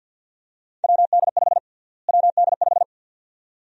Morse code, heard as a keyed tone, about 49 wpm, 700 Hz. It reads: WD5 WD5
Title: USA Call Sign Prefixes - Repeated 2x - 50wpm